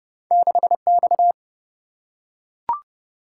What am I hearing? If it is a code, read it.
Morse code, heard as a keyed tone, about 29 wpm, 700 Hz. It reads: 6 X
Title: Single-Letter Single-Number 30wpm